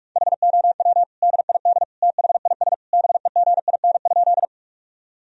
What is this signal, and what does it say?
Morse code, heard as a keyed tone, about 46 wpm, 700 Hz. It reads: HOW DID THIS BEGIN?
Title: Sentences from Top 1000 Words - Review Part 1 - 45wpm